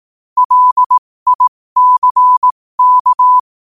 Morse code, heard as a keyed tone, about 18 wpm, spaced as for 15 wpm, 1000 Hz. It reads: LICK
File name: qotdfiles / 15wpm